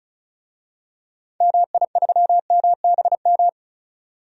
Morse code, heard as a keyed tone, about 36 wpm, 700 Hz. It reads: MI3MBM
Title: DXCC-100 Call Signs - 6 Characters - 35wpm